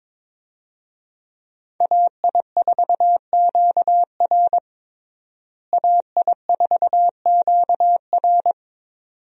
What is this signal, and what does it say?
Morse code, heard as a keyed tone, about 22 wpm, 700 Hz. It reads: AI4QR AI4QR
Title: USA Call Signs - 5 Characters - Repeated 2x - 22wpm